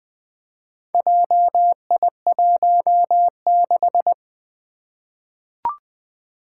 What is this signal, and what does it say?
Morse code, heard as a keyed tone, about 20 wpm, 700 Hz. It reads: JI16
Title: Four Challenging Letter-Number Combinations 20wpm